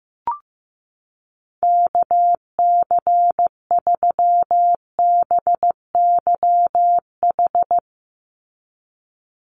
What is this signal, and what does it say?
Morse code, heard as a keyed tone, about 15 wpm, 700 Hz. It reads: KC3BYH
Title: USA Call Signs 15wpm